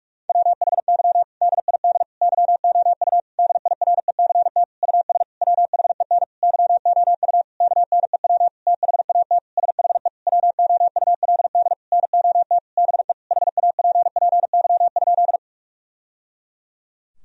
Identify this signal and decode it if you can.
Morse code, heard as a keyed tone, about 46 wpm, 700 Hz. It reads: WHY DID YOU DIRECT US WHEN YOU KNEW THAT SHE WOULD NOT BE HAPPY?
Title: Sentences from Top 1000 Words - Review Part 2 - 45wpm